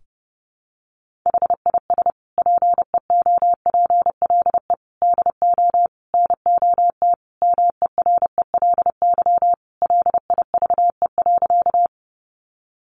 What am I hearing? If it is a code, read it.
Morse code, heard as a keyed tone, about 30 wpm, 700 Hz. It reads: HIS PEOPLE DO NOT MERELY LIVE.
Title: Sentences from Top 1400 Words 30wpm